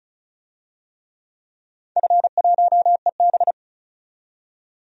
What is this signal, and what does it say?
Morse code, heard as a keyed tone, about 36 wpm, 690 Hz. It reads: F1EB
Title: DXCC-100 Call Signs - 4 Characters - 35wpm